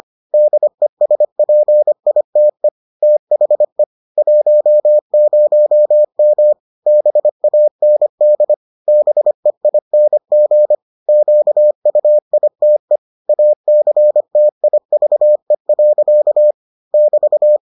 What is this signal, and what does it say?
Morse code, heard as a keyed tone, about 25 wpm, 600 Hz. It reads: DESPITE THE 10M BAND BEING QUITE ACTIVE. <BT>